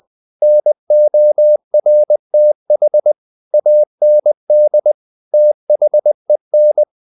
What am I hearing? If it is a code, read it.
Morse code, heard as a keyed tone, about 20 wpm, 600 Hz. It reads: NORTH AND THEN
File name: the old man and the sea / the old man and the sea part 1 20wpm